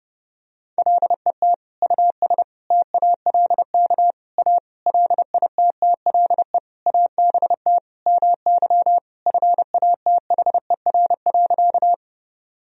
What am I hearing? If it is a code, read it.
Morse code, heard as a keyed tone, about 28 wpm, 700 Hz. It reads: LET US TALK A LITTLE ABT MY FATHER.
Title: Sentences from Top 400 Words and Abbreviations 30wpm